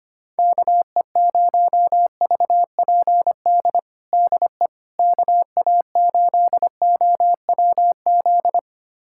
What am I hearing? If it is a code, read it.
Morse code, heard as a keyed tone, about 25 wpm, 700 Hz. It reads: KE0VPD DE KA8OWZ
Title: USA Callsign DE Callsign - 6 Characters - 25wpm